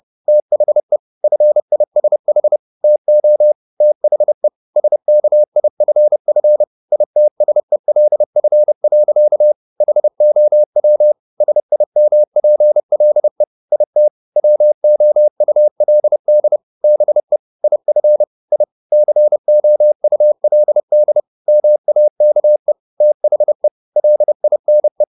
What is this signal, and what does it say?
Morse code, heard as a keyed tone, about 30 wpm, 600 Hz. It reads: THE FISH TO THE SKIFF ITSELF. HOW SIMPLE IT WOULD BE IF I COULD MAKE THE LINE